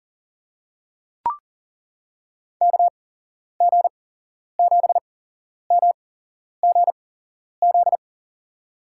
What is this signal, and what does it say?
Morse code, heard as a keyed tone, about 40 wpm, spaced as for 11 wpm, 700 Hz. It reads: KG7MGZ
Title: USA Call Signs - 6 Characters - 8x Character Spacing - 40wpm